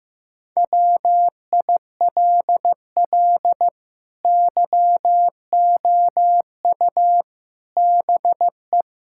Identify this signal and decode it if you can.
Morse code, heard as a keyed tone, about 15 wpm, 700 Hz. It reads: WILL YOU BE
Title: Sentences from Top 1600 Words 15wpm